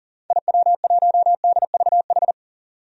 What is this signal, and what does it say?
Morse code, heard as a keyed tone, about 37 wpm, 700 Hz. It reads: IW1DVH
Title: DXCC-100 Call Signs - 6 Characters - 40wpm